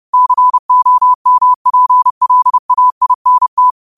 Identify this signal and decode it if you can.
Morse code, heard as a keyed tone, about 30 wpm, 1000 Hz. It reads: COMPLAINT
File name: qotdfiles / 30wpm